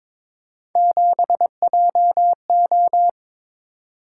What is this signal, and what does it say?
Morse code, heard as a keyed tone, about 22 wpm, 700 Hz. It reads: 7JO
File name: practice / 22wpm / Three Letter-Number Combinations - 22wpm